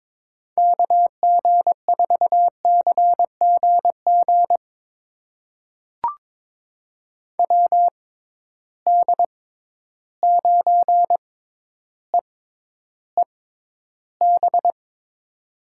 Morse code, heard as a keyed tone, about 22 wpm, spaced as for 11 wpm, 700 Hz. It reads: KG4CGG WD9EEB
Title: USA Call Signs - 6 Characters - 6x Character Spacing - 22wpm